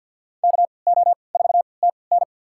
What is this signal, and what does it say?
Morse code, heard as a keyed tone, about 46 wpm, spaced as for 27 wpm, 700 Hz. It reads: KO4TN
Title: USA Call Signs - 5 Characters - 3x Character Spacing - 50wpm